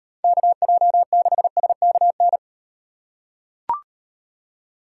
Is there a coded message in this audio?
Yes: Morse code from 0.2 to 2.4 s, heard as a keyed tone, about 38 wpm, 700 Hz. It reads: KJ6SKN